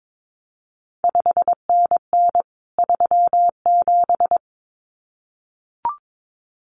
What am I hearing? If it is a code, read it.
Morse code, heard as a keyed tone, about 22 wpm, 700 Hz. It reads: 5NN 37
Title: CQ WW DX Contest Exchange Encore - 22wpm